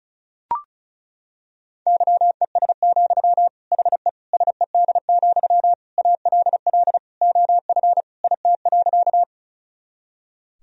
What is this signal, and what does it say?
Morse code, heard as a keyed tone, about 36 wpm, 700 Hz. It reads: YES, HE SED, ALL OF IT.
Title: Sentences from Top 500 Words and Abbreviations 35wpm